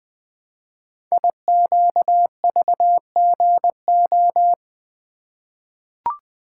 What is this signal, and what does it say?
Morse code, heard as a keyed tone, about 20 wpm, 700 Hz. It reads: IQVGO
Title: Code Groups - 5 Letters - 20wpm